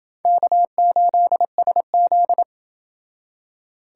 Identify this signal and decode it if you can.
Morse code, heard as a keyed tone, about 28 wpm, 700 Hz. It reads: K8SZ